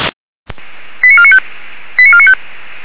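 Five-tone selective call (ZVEI), heard as quick runs of five tones, tones of about 70 ms. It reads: 89495 89495
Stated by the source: ZVEI five-tone call received by SDR.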